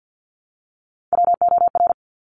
Morse code, heard as a keyed tone, about 50 wpm, 700 Hz. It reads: WOR
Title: Top Three Letter Combinations in Words 50wpm